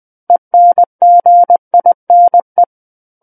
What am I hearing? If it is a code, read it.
Morse code, heard as a keyed tone, about 20 wpm, 700 Hz. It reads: ENGINE
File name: 20 WPM